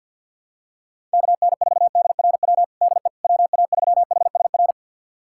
Morse code, heard as a keyed tone, about 48 wpm, 700 Hz. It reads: KN4DRW DE WA3HSR
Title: USA Callsign DE Callsign 50wpm